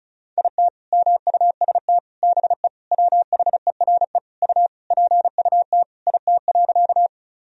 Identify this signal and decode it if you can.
Morse code, heard as a keyed tone, about 36 wpm, 700 Hz. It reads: IT MUST BE WHERE U PUT IT.
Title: Sentences from Top 200 Words and Abbreviations 35wpm